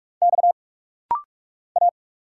Morse code, heard as a keyed tone, about 46 wpm, 690 Hz. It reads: X A
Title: Single Letters - Rapid-Fire 45wpm